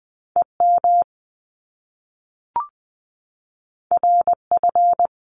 Morse code, heard as a keyed tone, about 20 wpm, 700 Hz. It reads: E M R F
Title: Top Two Letter Combinations in Words 20wpm